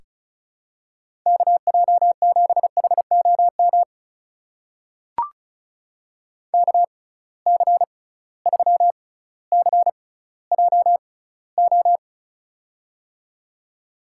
Morse code, heard as a keyed tone, about 34 wpm, spaced as for 18 wpm, 700 Hz. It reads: KJ7HOM KC3CJO
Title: USA Call Signs - 6 Characters - 6x Character Spacing - 35wpm